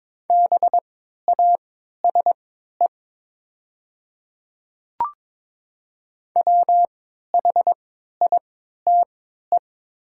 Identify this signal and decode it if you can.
Morse code, heard as a keyed tone, about 22 wpm, spaced as for 12 wpm, 700 Hz. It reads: BASE WHITE
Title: 3x Character Spacing - Top 500 Words - 22wpm